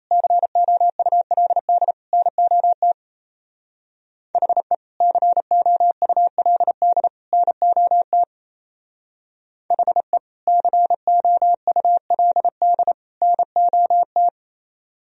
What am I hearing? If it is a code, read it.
Morse code, heard as a keyed tone, about 33 wpm, 700 Hz. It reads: COULD NOT HE COULD NOT HE COULD NOT
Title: Speed-Racing - Sets of 3 Words 28wpm